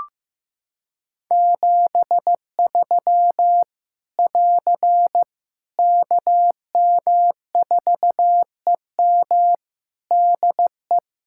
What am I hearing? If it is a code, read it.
Morse code, heard as a keyed tone, about 15 wpm, 700 Hz. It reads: 73 <AR> KM4EM DE
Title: Practice QSO - Exchange 4 - Ending - 15wpm